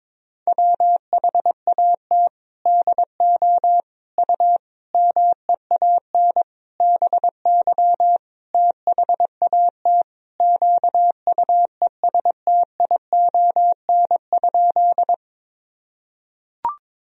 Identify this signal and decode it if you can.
Morse code, heard as a keyed tone, about 22 wpm, 700 Hz. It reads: WHAT DO U MEAN BY THAT QUESTION?
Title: Sentences from Top 400 Words and Abbreviations 22wpm